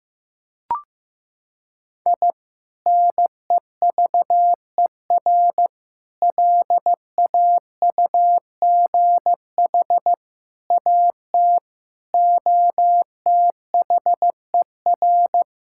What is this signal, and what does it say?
Morse code, heard as a keyed tone, about 15 wpm, 700 Hz. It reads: I NEVER LAUGH AT OTHER
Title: Sentences from Top 1000 Words - Review Part 6 - 15wpm